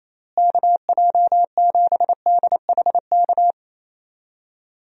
Morse code, heard as a keyed tone, about 28 wpm, 700 Hz. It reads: KJ7DHK
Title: USA Call Signs - 6 Characters - Repeated 2x - 28wpm